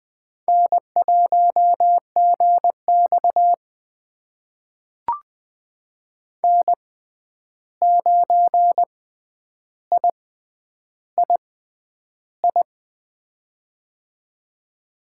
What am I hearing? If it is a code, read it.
Morse code, heard as a keyed tone, about 20 wpm, spaced as for 10 wpm, 700 Hz. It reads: N1GX N9III
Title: USA Call Signs - 6x Character Spacing - 20wpm